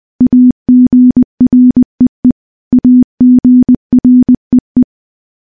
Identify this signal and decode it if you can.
Morse code, heard as a keyed tone, about 19 wpm, 250 Hz. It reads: AGREE AGREE
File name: morse / morse-training-20WPM